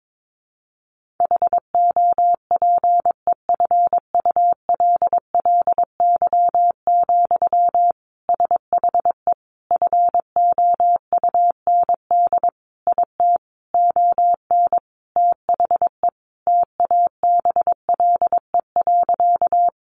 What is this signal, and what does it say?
Morse code, heard as a keyed tone, about 22 wpm, 700 Hz. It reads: HOPEFULLY, SHE FOUND IT ON THE TABLE.
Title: Sentences from Top 1500 Words 22wpm